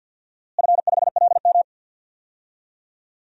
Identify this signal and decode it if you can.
Morse code, heard as a keyed tone, about 50 wpm, 700 Hz. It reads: F5LM
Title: DXCC-100 Call Signs - 4 Characters - Repeated 3x - 50wpm